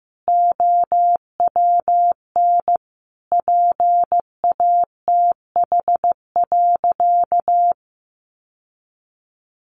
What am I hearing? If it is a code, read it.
Morse code, heard as a keyed tone, about 15 wpm, 700 Hz. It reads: OWN PATH.
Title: Sentences from Top 1400 Words 15wpm